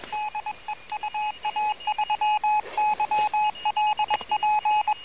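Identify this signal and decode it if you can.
Morse code, heard as a keyed tone, about 21 wpm, 850 Hz, with radio noise. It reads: DE UA3YLP
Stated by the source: QRP contact between UA1CEG and UA3YLP using homemade gear on 14060 kHz during a QRP round table, with heavy interference from a contest.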